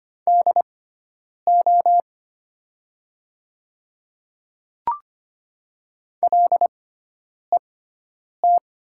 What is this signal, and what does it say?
Morse code, heard as a keyed tone, about 25 wpm, spaced as for 9 wpm, 700 Hz. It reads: DO LET
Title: 6x Character Spacing - Top 300 Words - 25wpm